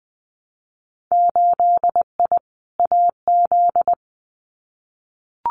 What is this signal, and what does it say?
Morse code, heard as a keyed tone, about 20 wpm, 700 Hz. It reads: 8I AZ
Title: Winter Field Day Exchanges Encore 20wpm